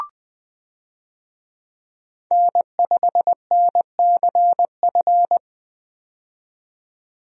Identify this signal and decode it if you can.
Morse code, heard as a keyed tone, about 20 wpm, 700 Hz. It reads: N5NCF
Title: USA Call Signs - 5 Characters - 20wpm